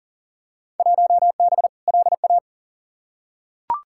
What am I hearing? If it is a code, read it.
Morse code, heard as a keyed tone, about 40 wpm, 700 Hz. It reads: 1B LA
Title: Field Day Exchanges - 40wpm